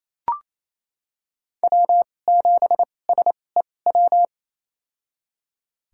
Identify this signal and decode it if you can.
Morse code, heard as a keyed tone, about 28 wpm, spaced as for 20 wpm, 700 Hz. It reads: W7SEW
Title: USA Call Signs - 5 Characters - 2x Character Spacing - 28wpm